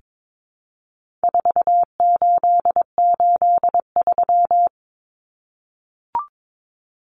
Morse code, heard as a keyed tone, about 22 wpm, 700 Hz. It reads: 4883
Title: Numbers - 4-Digit 22wpm